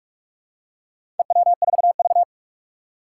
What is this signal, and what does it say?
Morse code, heard as a keyed tone, about 42 wpm, 700 Hz. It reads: EW4V
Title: Four Challenging Letter-Number Combinations 45wpm